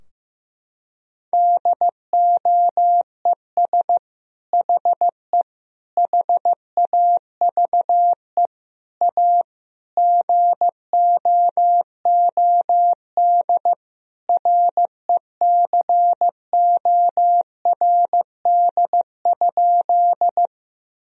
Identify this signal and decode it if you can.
Morse code, heard as a keyed tone, about 16 wpm, 700 Hz. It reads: DOES HE HAVE A GOOD RECORD?
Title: Sentences from Top 1000 Words - Review Part 3 - 15wpm